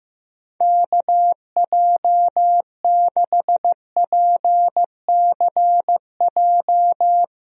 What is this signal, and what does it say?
Morse code, heard as a keyed tone, about 15 wpm, 700 Hz. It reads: KJ6PCJ
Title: USA Call Signs - 6 Characters - Repeated 2x - 15wpm